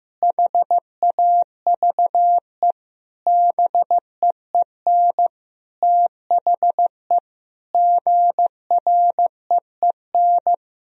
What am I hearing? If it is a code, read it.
Morse code, heard as a keyed tone, about 15 wpm, 700 Hz. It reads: HAVE BEEN THE GREEN
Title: Sentences from Top 1000 Words - Review Part 2 - 15wpm